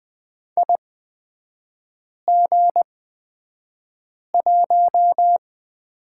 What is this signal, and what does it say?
Morse code, heard as a keyed tone, about 20 wpm, spaced as for 5 wpm, 700 Hz. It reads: IG1
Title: 236 Lesson 38e - Letter q - Three Characters 20wpm